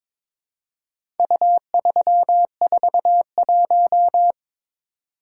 Morse code, heard as a keyed tone, about 22 wpm, 700 Hz. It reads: U341
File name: practice / 22wpm / Four Letter-Number Combinations - 22wpm